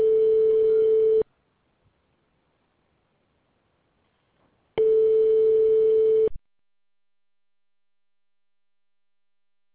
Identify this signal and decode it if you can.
Ringback tone (France), 440 Hz.